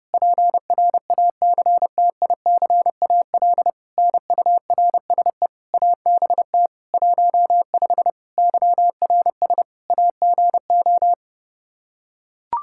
Morse code, heard as a keyed tone, about 30 wpm, 700 Hz. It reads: PRACTICAL NURSE ABT 15 YRS AGO
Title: K1USN SST QSO Practice 30wpm